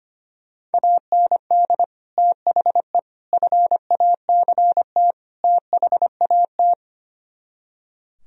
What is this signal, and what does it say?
Morse code, heard as a keyed tone, about 29 wpm, 700 Hz. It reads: AND THE FACT THAT
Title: Speed-Racing - Sets of 4 Words 25wpm